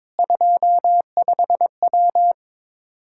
Morse code, heard as a keyed tone, about 22 wpm, 700 Hz. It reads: 25W
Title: Practice QSO - Exchange 2 - Info - 22wpm